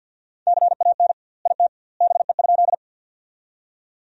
Morse code, heard as a keyed tone, about 47 wpm, 700 Hz. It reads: CAN IT BE?
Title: 2x Word-Spacing - Sentences from Top 100 Words - 50wpm